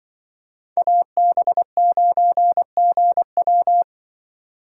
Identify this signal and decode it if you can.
Morse code, heard as a keyed tone, about 24 wpm, 700 Hz. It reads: AB9GW
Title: Speed Racing - USA 2x2 Call Signs 20wpm